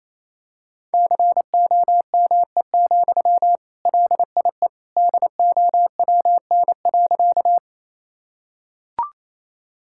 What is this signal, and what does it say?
Morse code, heard as a keyed tone, about 28 wpm, 700 Hz. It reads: COME, LIE DOWN.
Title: Sentences from Top 1600 Words 28wpm